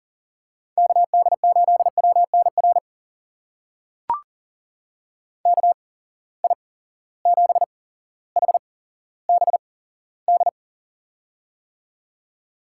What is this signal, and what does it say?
Morse code, heard as a keyed tone, about 40 wpm, spaced as for 17 wpm, 700 Hz. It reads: KD8WNR KI7HBD
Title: USA Call Signs - 6 Characters - 8x Character Spacing - 40wpm